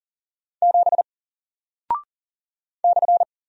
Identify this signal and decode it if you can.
Morse code, heard as a keyed tone, about 40 wpm, 700 Hz. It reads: 7 /
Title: Single Character (Letter:Number:Punctuation:Prosign) Alt 40wpm